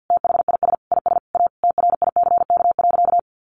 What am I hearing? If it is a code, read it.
Morse code, heard as a keyed tone, about 48 wpm, 700 Hz. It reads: THIS IS A TRICK.
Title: Sentences from Top 1500 Words 50wpm